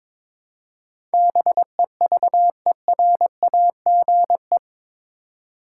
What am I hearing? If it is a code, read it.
Morse code, heard as a keyed tone, about 22 wpm, 700 Hz. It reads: BEVERAGE
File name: practice / 22wpm / Eight Letter Words - 22wpm